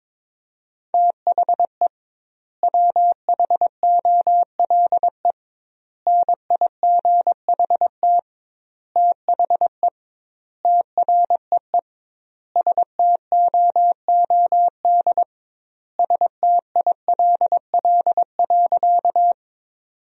Morse code, heard as a keyed tone, about 22 wpm, 700 Hz. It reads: THE WHOLE NIGHT THE TREE STOOD STILL.
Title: 2x Word-Spacing - Sentences from Top 500 Words - 22wpm